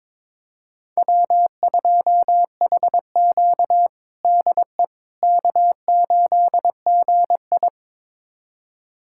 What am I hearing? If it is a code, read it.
Morse code, heard as a keyed tone, about 22 wpm, 700 Hz. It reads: W2HQ DE K8GI
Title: USA Callsign DE Callsign - 4 Characters - 22wpm